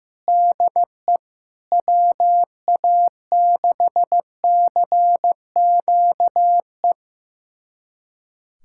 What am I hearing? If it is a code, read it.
Morse code, heard as a keyed tone, about 15 wpm, 700 Hz. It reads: DE WA6CQE
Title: USA Callsign DE Callsign 15wpm